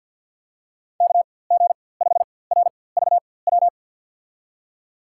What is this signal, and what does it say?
Morse code, heard as a keyed tone, about 53 wpm, spaced as for 24 wpm, 700 Hz. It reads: KG5RVW
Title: USA Call Signs - 4x Character Spacing - 50wpm